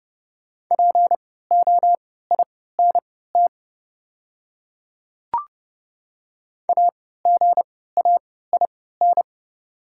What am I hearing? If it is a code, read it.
Morse code, heard as a keyed tone, about 29 wpm, spaced as for 17 wpm, 700 Hz. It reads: POINT AGAIN
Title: 3x Character Spacing - Top 200 Words - 30wpm